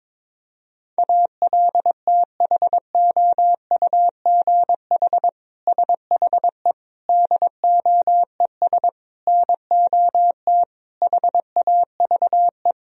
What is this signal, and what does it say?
Morse code, heard as a keyed tone, about 22 wpm, 700 Hz. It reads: ALTHOUGH SHE DOES NOT HAVE
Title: Sentences from Top 1000 Words - Review Part 3 - 22wpm